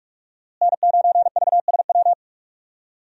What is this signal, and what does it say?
Morse code, heard as a keyed tone, about 46 wpm, 700 Hz. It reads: N9VSW
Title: USA Call Signs - 5 Characters - Repeated 2x - 45wpm